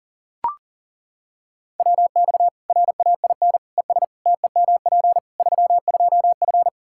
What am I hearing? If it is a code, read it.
Morse code, heard as a keyed tone, about 40 wpm, 700 Hz. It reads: WX RAIN ES TEMP 32F
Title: K1USN SST QSO Practice 40wpm